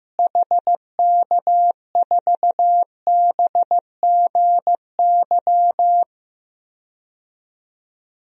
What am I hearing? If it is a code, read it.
Morse code, heard as a keyed tone, about 15 wpm, 700 Hz. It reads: HK4BGY
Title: DXCC-100 Call Signs - 6 Characters - Repeated 3x - 15wpm